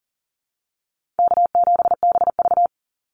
Morse code, heard as a keyed tone, about 40 wpm, 700 Hz. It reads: K7BV
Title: Call Signs Repeated 2x - 40wpm